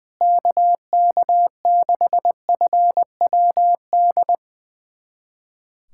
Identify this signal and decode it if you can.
Morse code, heard as a keyed tone, about 21 wpm, 700 Hz. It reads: KK6FWD
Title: USA Callsign DE Callsign 20wpm